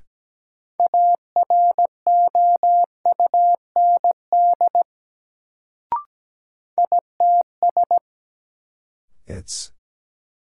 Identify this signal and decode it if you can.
Morse code, heard as a keyed tone, about 17 wpm, 700 Hz. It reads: AROUND ITS
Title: Sentences from Top 600 Words - New Word Drill - 17wpm